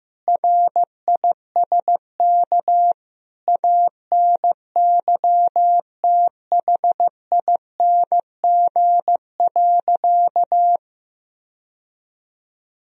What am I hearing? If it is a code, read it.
Morse code, heard as a keyed tone, about 16 wpm, 700 Hz. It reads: RISK ANYTHING.
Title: Sentences from Top 1000 Words - Review Part 4 - 15wpm